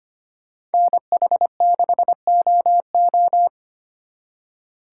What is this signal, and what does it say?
Morse code, heard as a keyed tone, about 32 wpm, 700 Hz. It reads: NH6OO